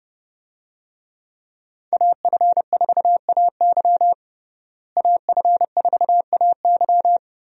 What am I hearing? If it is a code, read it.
Morse code, heard as a keyed tone, about 30 wpm, 700 Hz. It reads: AF4AY AF4AY
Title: USA Call Signs - 5 Characters - Repeated 3x - 30wpm